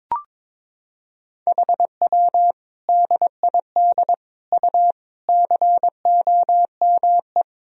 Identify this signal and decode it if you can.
Morse code, heard as a keyed tone, about 22 wpm, 700 Hz. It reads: HW DID U COME
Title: Sentences from Top 100 Words and Abbreviations 22wpm